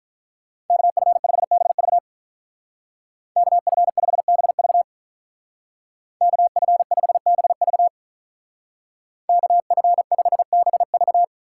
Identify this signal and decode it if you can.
Morse code, heard as a keyed tone, about 43 wpm, 700 Hz. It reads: KF5BV KF5BV KF5BV KF5BV
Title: Speed Racing - USA 2x2 Call Signs 35wpm